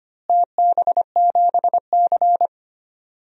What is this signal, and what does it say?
Morse code, heard as a keyed tone, about 25 wpm, 700 Hz. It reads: TB7C
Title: DXCC-100 Call Signs - 4 Characters - Repeated 3x - 25wpm